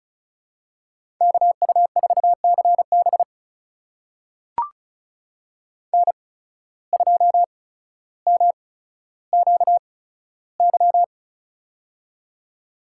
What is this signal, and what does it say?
Morse code, heard as a keyed tone, about 34 wpm, spaced as for 15 wpm, 700 Hz. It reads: KU4CB N2MQY
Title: USA Call Signs - 5 Characters - 8x Character Spacing - 35wpm